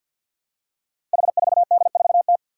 Morse code, heard as a keyed tone, about 53 wpm, 700 Hz. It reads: H4D4T
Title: Five Challenging Letter-Number Combinations 50wpm